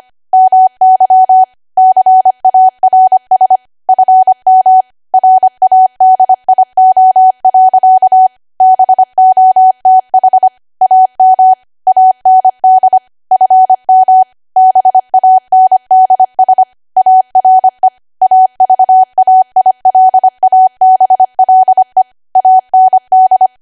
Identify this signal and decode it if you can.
Morse code, heard as a keyed tone, about 25 wpm, 750 Hz. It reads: MY CARS FM RADIO. BOTH AM AND FM BANDS ARE AVAILABLE AND
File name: Archive / 25 WPM / 25WPM